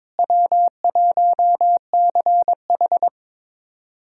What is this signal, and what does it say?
Morse code, heard as a keyed tone, about 22 wpm, 700 Hz. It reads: W1CH
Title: USA Callsign DE Callsign - 4 Characters - 22wpm